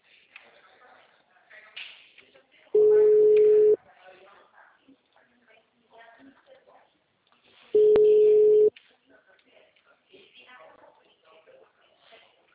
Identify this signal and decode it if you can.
Ringback tone (Europe, 425 Hz), with speech.
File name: out- -2002- - - .29430